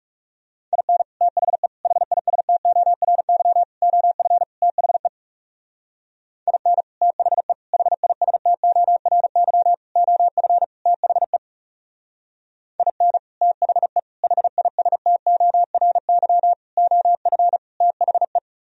Morse code, heard as a keyed tone, about 40 wpm, 700 Hz. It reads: IN THE HISTORY OF THE IN THE HISTORY OF THE IN THE HISTORY OF THE